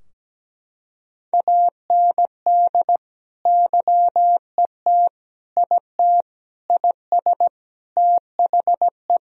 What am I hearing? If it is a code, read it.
Morse code, heard as a keyed tone, about 17 wpm, 700 Hz. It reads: AND YET IT IS THE
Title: Sentences from Top 1000 Words - Review Part 3 - 17wpm